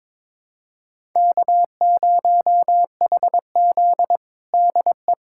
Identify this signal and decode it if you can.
Morse code, heard as a keyed tone, about 22 wpm, 700 Hz. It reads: K0HZ DE
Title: USA Callsign DE Callsign - 4 Characters - 22wpm